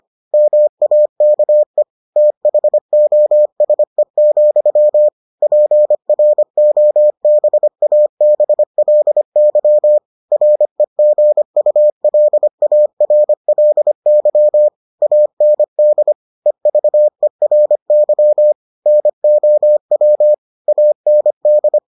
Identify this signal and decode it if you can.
Morse code, heard as a keyed tone, about 25 wpm, 600 Hz. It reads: MAKE THOSE, PROBABLY REGULARLY AND EVERY NOW AND